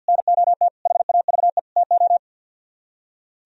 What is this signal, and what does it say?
Morse code, heard as a keyed tone, about 64 wpm, 700 Hz. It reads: NOT HAVE TO